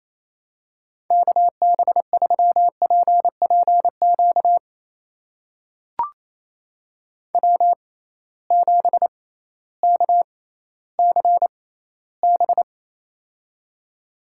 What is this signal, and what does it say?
Morse code, heard as a keyed tone, about 28 wpm, spaced as for 15 wpm, 700 Hz. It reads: KB3PPQ W7KCB
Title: USA Call Signs - 6x Character Spacing - 28wpm